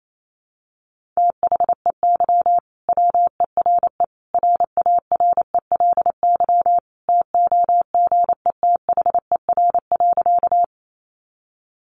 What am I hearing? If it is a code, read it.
Morse code, heard as a keyed tone, about 28 wpm, 700 Hz. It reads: THEY WERE RARELY TOGETHER.
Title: Sentences from Top 1400 Words 28wpm